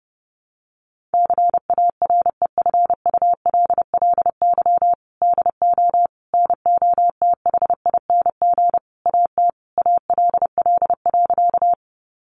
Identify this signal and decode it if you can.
Morse code, heard as a keyed tone, about 28 wpm, 700 Hz. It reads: CAREFULLY DO NOTHING AT ALL.
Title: Sentences from Top 1300 Words 30wpm